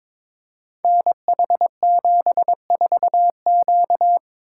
Morse code, heard as a keyed tone, about 22 wpm, 700 Hz. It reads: NH74Q
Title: Code Groups - 5 Letter:Numbers - 22wpm